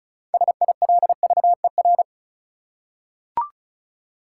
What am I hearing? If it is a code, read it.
Morse code, heard as a keyed tone, about 36 wpm, 700 Hz. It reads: SILVER
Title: Six Letter Words - 35wpm